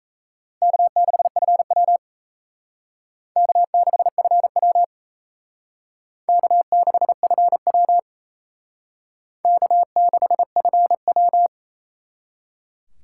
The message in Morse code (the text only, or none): K6FW K6FW K6FW K6FW